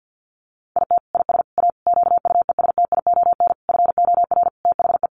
Morse code, heard as a keyed tone, about 50 wpm, 700 Hz. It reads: IT IS A QUESTION FOR THE
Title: Sentences from Top 1300 Words 50wpm